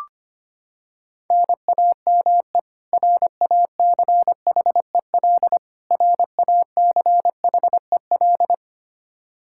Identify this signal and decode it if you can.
Morse code, heard as a keyed tone, about 25 wpm, 700 Hz. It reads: NAME RACHEL RACHEL
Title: Practice QSO - Exchange 1 - Beginning - 25wpm